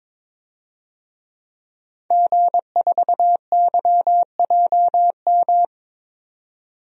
Morse code, heard as a keyed tone, about 21 wpm, 700 Hz. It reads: G4YJM
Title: DXCC-100 Call Signs - 5 Characters - 22wpm